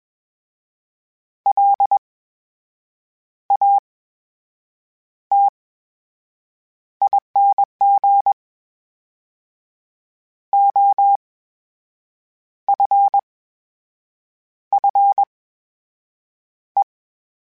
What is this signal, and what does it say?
Morse code, heard as a keyed tone, about 21 wpm, spaced as for 6 wpm, 800 Hz. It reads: LATING OFFE
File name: Special INGs-mp3s / Special INGs-mp3s / VerySlowToFaster Ing @20